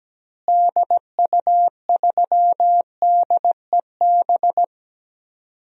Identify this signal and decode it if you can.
Morse code, heard as a keyed tone, about 17 wpm, 700 Hz. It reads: DU3DEB
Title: DXCC-100 Call Signs - 6 Characters - Repeated 3x - 17wpm